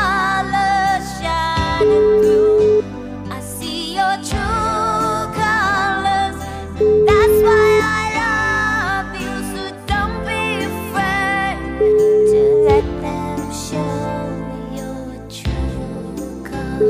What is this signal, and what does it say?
Ringback tone (Europe, 425 Hz), with noise.